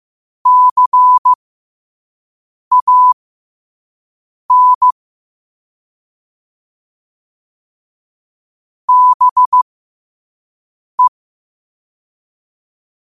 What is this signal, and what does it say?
Morse code, heard as a keyed tone, about 15 wpm, spaced as for 5 wpm, 1000 Hz. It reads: CAN BE
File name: qotdfiles / 5wpm